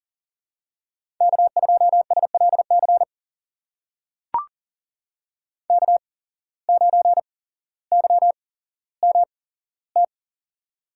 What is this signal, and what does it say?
Morse code, heard as a keyed tone, about 40 wpm, spaced as for 17 wpm, 700 Hz. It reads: K2SLC K9YMT
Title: USA Call Signs - 5 Characters - 8x Character Spacing - 40wpm